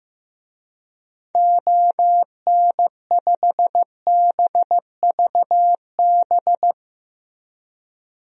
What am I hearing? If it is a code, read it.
Morse code, heard as a keyed tone, about 14 wpm, 700 Hz. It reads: ON5BVB
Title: DXCC-100 Call Signs - 6 Characters - Repeated 3x - 15wpm